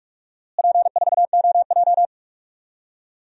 Morse code, heard as a keyed tone, about 46 wpm, 700 Hz. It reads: P4OJ
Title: DXCC-100 Call Signs - 4 Characters - Repeated 3x - 45wpm